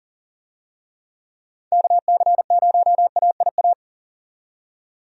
Morse code, heard as a keyed tone, about 40 wpm, 700 Hz. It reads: KC0AIA
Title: USA Call Signs - 6 Characters - 40wpm